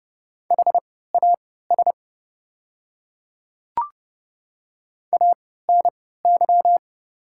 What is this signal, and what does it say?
Morse code, heard as a keyed tone, about 32 wpm, spaced as for 17 wpm, 690 Hz. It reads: HAS ANY
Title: 3x Character Spacing - Top 500 Words - 30wpm